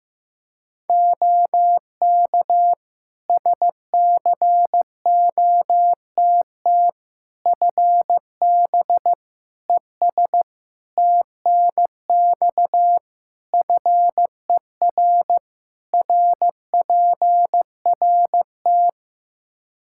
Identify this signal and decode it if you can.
Morse code, heard as a keyed tone, about 16 wpm, 700 Hz. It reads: OK SCOTT FB ES TNX FER RPRT
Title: Practice QSO - Exchange 2 - Info - 15wpm